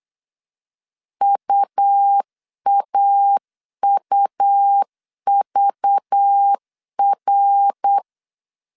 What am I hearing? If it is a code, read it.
Morse code, heard as a keyed tone, about 8 wpm, 800 Hz. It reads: UAUVR
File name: CW / DET / 1 9 1500C S A 8wpm 5637